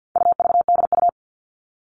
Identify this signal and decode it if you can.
Morse code, heard as a keyed tone, about 50 wpm, 700 Hz. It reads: UVDU
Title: All Cut Numbers - 4-Digit 50wpm